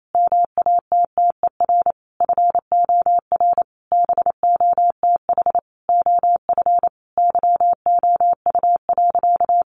Morse code, heard as a keyed tone, about 28 wpm, 700 Hz. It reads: MATTER FOR BOTH OF YOU.
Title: Sentences from Top 1100 Words 28wpm